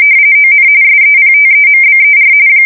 RTTY, about 45.45 baud. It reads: ON SEPTEMBER
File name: fsk170 / USS-New-Jersey